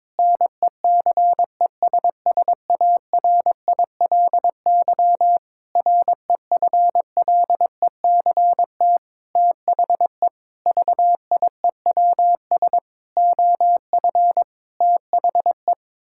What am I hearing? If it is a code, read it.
Morse code, heard as a keyed tone, about 22 wpm, 700 Hz. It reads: NECESSARILY REFLECT THE VIEWS OF THE
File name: practice / 22wpm / Sentences with 8-Word Phrases - 22wpm